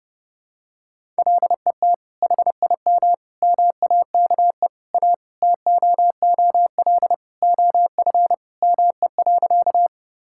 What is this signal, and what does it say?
Morse code, heard as a keyed tone, about 30 wpm, 700 Hz. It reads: LET HIM MAKE A TOOL OF ME.
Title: Sentences from Top 1000 Words - Review Part 6 - 30wpm